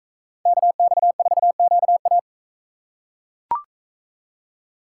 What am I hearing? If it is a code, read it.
Morse code, heard as a keyed tone, about 40 wpm, 700 Hz. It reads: KX4QA